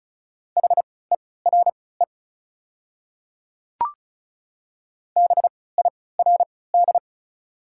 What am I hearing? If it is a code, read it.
Morse code, heard as a keyed tone, about 36 wpm, spaced as for 20 wpm, 700 Hz. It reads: HERE BIRD
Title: 3x Character Spacing - Top 500 Words - 35wpm